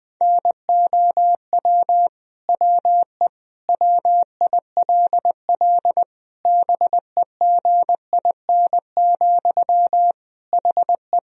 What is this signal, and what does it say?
Morse code, heard as a keyed tone, about 20 wpm, 700 Hz. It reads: NOW WE WILL BEGIN, HE
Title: Sentences from Top 1000 Words - Review Part 1 - 20wpm